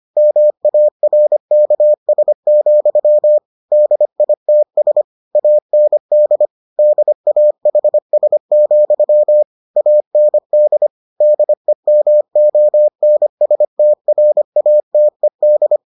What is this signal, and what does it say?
Morse code, heard as a keyed tone, about 25 wpm, 600 Hz. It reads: MARKS, DITS AND DAHS, AND DEMONSTRATED